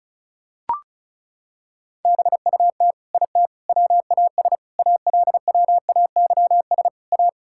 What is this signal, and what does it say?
Morse code, heard as a keyed tone, about 34 wpm, 700 Hz. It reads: BUT IT WAS ALWAYS A
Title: Sentences from Top 1600 Words 35wpm